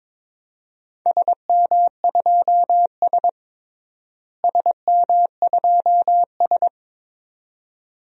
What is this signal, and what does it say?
Morse code, heard as a keyed tone, about 22 wpm, 700 Hz. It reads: SM2S SM2S
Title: DXCC-100 Call Signs - 4 Characters - Repeated 2x - 22wpm